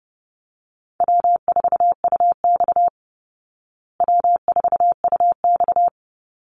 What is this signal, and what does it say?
Morse code, heard as a keyed tone, about 30 wpm, 700 Hz. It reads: W4UX W4UX
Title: Easy Call Signs Repeated 3x - 30wpm